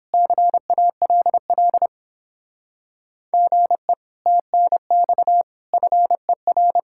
Morse code, heard as a keyed tone, about 29 wpm, 700 Hz. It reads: CALL GE TNX FER